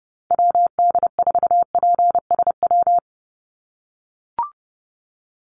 Morse code, heard as a keyed tone, about 30 wpm, 700 Hz. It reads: WD4PSW
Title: USA Call Signs 30wpm